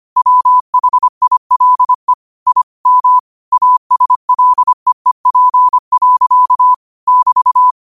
Morse code, heard as a keyed tone, about 25 wpm, 1000 Hz. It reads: WHILE I M ASLEEP. <BT>
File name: qotdfiles / 25wpm